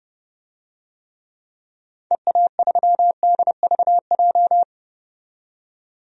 Morse code, heard as a keyed tone, about 32 wpm, 690 Hz. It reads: EA3DVJ
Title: DXCC-100 Call Signs - 6 Characters - 30wpm